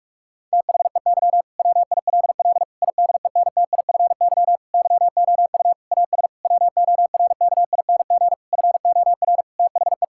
Morse code, heard as a keyed tone, about 41 wpm, 700 Hz. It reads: THEY WILL IDENTIFY YOU AS WORKING FOR THE
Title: Sentences from Top 1000 Words - Review Part 6 - 45wpm